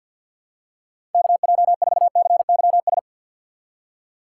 Morse code, heard as a keyed tone, about 48 wpm, 700 Hz. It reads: KJ4CYS
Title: USA Call Signs - 3x Character Spacing - 50wpm